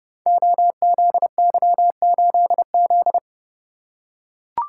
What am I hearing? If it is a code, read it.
Morse code, heard as a keyed tone, about 30 wpm, 700 Hz. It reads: OZY8Z
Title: Code Groups - 5 Letter:Numbers - 30wpm